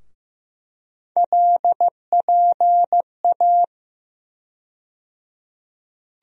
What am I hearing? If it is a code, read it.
Morse code, heard as a keyed tone, about 16 wpm, 700 Hz. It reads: LPA
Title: USA Call Sign Postfixes - 3 Letters 15wpm